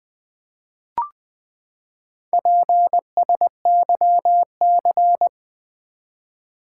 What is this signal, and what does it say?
Morse code, heard as a keyed tone, about 20 wpm, 700 Hz. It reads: PSYC
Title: Build A Word - 4 Letters At A Time - 12-Letter Words - 20wpm